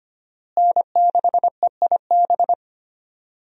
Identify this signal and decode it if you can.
Morse code, heard as a keyed tone, about 25 wpm, 700 Hz. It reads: N6EIB
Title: USA Call Signs - 5 Characters - Repeated 3x - 25wpm